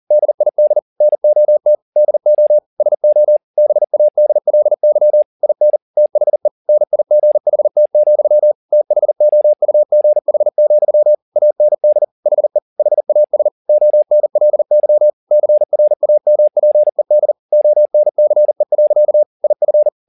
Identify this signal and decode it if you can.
Morse code, heard as a keyed tone, about 40 wpm, 600 Hz. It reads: DID NOT DO SO BADLY IN THE NIGHT, THOUGH, AND HE HAS ONLY CRAMPED ONCE. IF
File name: the old man and the sea 2 40wpm